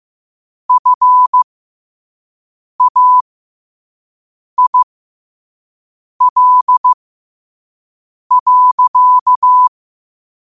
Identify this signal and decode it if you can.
Morse code, heard as a keyed tone, about 15 wpm, spaced as for 5 wpm, 1000 Hz. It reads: FAIL.
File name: 5wpm